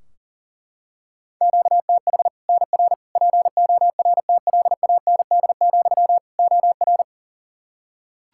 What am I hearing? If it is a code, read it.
Morse code, heard as a keyed tone, about 40 wpm, 700 Hz. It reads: QTH NR PORTLAND, OR
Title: Practice QSO - Exchange 1 - Beginning - 40wpm